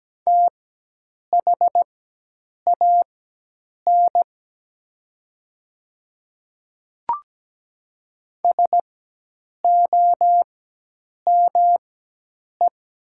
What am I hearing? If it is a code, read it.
Morse code, heard as a keyed tone, about 17 wpm, spaced as for 8 wpm, 700 Hz. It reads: THAN SOME
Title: 4x Character Spacing - Top 200 Words - 17wpm